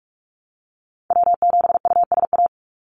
Morse code, heard as a keyed tone, about 44 wpm, 700 Hz. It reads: W7USA
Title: NA SOTA Activator Callsigns 45wpm